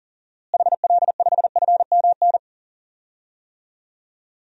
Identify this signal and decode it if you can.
Morse code, heard as a keyed tone, about 40 wpm, 700 Hz. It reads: HL5FMN
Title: DXCC-100 Call Signs - 6 Characters - Repeated 2x - 40wpm